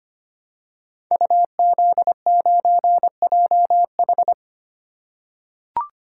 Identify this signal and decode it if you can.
Morse code, heard as a keyed tone, about 25 wpm, 700 Hz. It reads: UZ9JH
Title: Code Groups - 5 Letter:Numbers - 25wpm